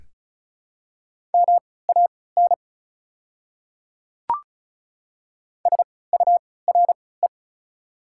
Morse code, heard as a keyed tone, about 34 wpm, spaced as for 20 wpm, 700 Hz. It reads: MAN SURE
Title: 3x Character Spacing - Top 300 Words - 35wpm